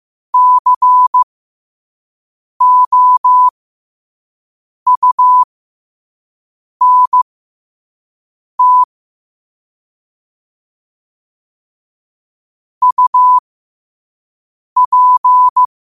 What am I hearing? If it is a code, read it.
Morse code, heard as a keyed tone, about 15 wpm, spaced as for 5 wpm, 1000 Hz. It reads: COUNT UP
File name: qotdfiles / 5wpm